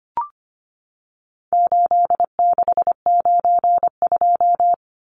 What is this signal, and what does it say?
Morse code, heard as a keyed tone, about 25 wpm, 700 Hz. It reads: 8692
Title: Numbers - 4-Digit 25wpm